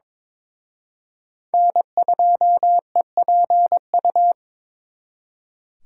Morse code, heard as a keyed tone, about 26 wpm, 700 Hz. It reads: N2EPU
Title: Speed Racing - USA 1x3 Call Signs 22wpm